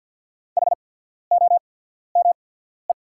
Morse code, heard as a keyed tone, about 48 wpm, spaced as for 14 wpm, 700 Hz. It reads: HOME
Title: 8x Character Spacing - Top 200 Words - 50wpm